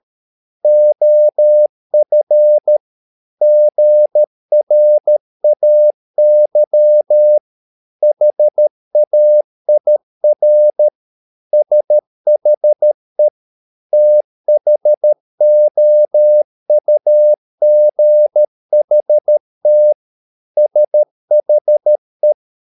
Morse code, heard as a keyed tone, about 13 wpm, 600 Hz. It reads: OF GRAY HAIR SHE THOUGHT SHE